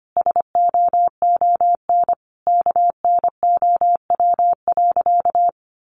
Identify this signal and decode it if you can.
Morse code, heard as a keyed tone, about 25 wpm, 700 Hz. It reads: SOON KNOW.
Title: Sentences from Top 400 Words 25wpm